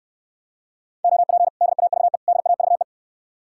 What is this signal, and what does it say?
Morse code, heard as a keyed tone, about 69 wpm, 700 Hz. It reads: OP DAVE DAVE